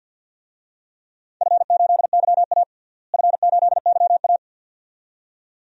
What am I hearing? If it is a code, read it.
Morse code, heard as a keyed tone, about 53 wpm, 700 Hz. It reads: F8YA F8YA
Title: DXCC-100 Call Signs - 4 Characters - Repeated 2x - 50wpm